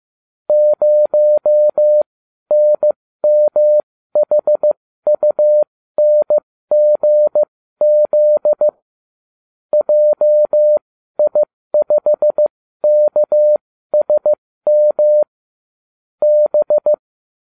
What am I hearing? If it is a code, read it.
Morse code, heard as a keyed tone, about 15 wpm, spaced as for 12 wpm, 600 Hz. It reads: 0NMHUNGZ JI5KSM B
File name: cw / ubacw30 3